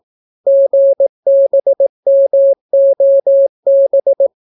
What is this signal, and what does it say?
Morse code, heard as a keyed tone, about 18 wpm, 550 Hz. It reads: GBMOB